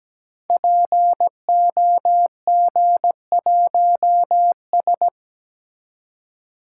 Morse code, heard as a keyed tone, about 17 wpm, 700 Hz. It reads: POG1S
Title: Code Groups - 5 Letter:Numbers - 17wpm